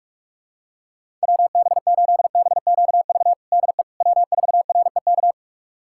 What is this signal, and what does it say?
Morse code, heard as a keyed tone, about 42 wpm, 700 Hz. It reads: WB8BQV DE W4REK
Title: USA Callsign DE Callsign 45wpm